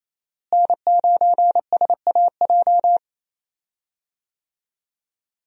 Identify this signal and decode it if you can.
Morse code, heard as a keyed tone, about 28 wpm, 700 Hz. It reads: N9SAJ
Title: USA Call Signs - 5 Characters - Repeated 3x - 28wpm